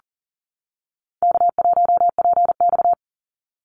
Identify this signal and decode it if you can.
Morse code, heard as a keyed tone, about 40 wpm, 700 Hz. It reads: K1PX
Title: Call Signs - 40wpm